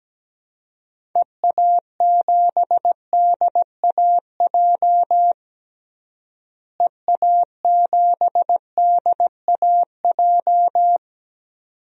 Morse code, heard as a keyed tone, about 17 wpm, 700 Hz. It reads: EA7DAJ EA7DAJ
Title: DXCC-100 Call Signs - 6 Characters - Repeated 2x - 17wpm